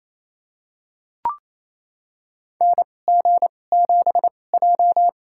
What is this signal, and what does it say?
Morse code, heard as a keyed tone, about 27 wpm, spaced as for 20 wpm, 700 Hz. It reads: NG7J
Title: USA Call Signs - 2x Character Spacing - 28wpm